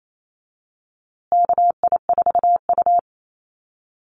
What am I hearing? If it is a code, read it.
Morse code, heard as a keyed tone, about 28 wpm, 700 Hz. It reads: KI4U
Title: USA Call Signs Repeated 3x 28wpm